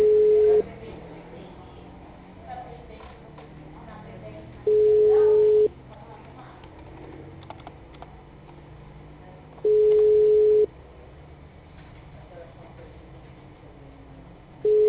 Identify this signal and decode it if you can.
Ringback tone (Europe, 425 Hz), with speech.
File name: out- -2019- - - .16661